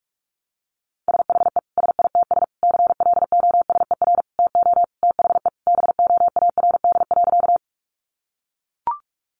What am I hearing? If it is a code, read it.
Morse code, heard as a keyed tone, about 44 wpm, 700 Hz. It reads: SHE SITS CLOSER TO THE BOARD.
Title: Additional Easy Phrases 45wpm